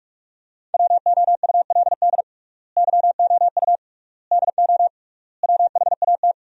Morse code, heard as a keyed tone, about 46 wpm, 700 Hz. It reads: WOULD YOU DO WHAT
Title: 3x Word-Spacing - Sentences from Top 100 Words - 45wpm